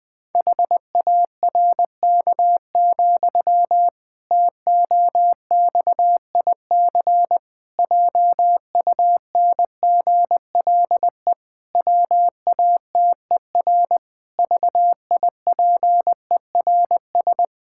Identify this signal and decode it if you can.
Morse code, heard as a keyed tone, about 20 wpm, 700 Hz. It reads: HARK, TOXIC JUNGLE WATER VIPERS
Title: Pangram Challenge v2 - 20wpm